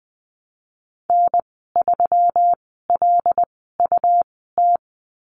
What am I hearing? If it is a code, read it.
Morse code, heard as a keyed tone, about 19 wpm, spaced as for 14 wpm, 700 Hz. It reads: N3LUT
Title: 167 Lesson 27g - Number 7 - Call Signs 20wpm